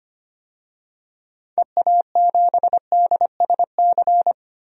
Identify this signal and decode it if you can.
Morse code, heard as a keyed tone, about 25 wpm, 700 Hz. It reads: EA7DSC
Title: DXCC-100 Call Signs - 6 Characters - 25wpm